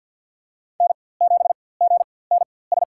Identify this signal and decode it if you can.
Morse code, heard as a keyed tone, about 47 wpm, spaced as for 23 wpm, 700 Hz. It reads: N7GNS
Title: USA Call Signs - 5 Characters - 4x Character Spacing - 50wpm